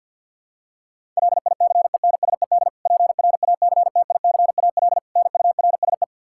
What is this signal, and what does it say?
Morse code, heard as a keyed tone, about 52 wpm, 700 Hz. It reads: LICENSED PRACTICAL NURSE
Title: Practice QSO - Exchange 3 - More Info - 50wpm